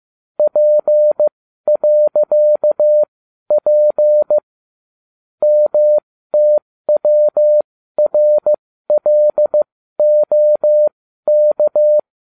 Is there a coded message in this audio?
Yes: Morse code from 0.4 to 12.0 s, heard as a keyed tone, about 15 wpm, spaced as for 12 wpm, 600 Hz. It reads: P.P MTWRLOK